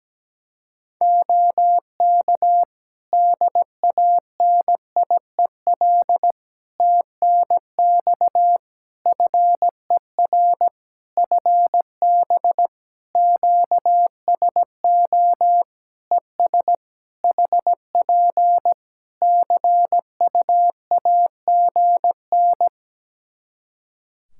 Morse code, heard as a keyed tone, about 17 wpm, 700 Hz. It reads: OK DANIEL TNX FER FB QSO ES HP CUAGN
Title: Practice QSO - Exchange 4 - Ending - 17wpm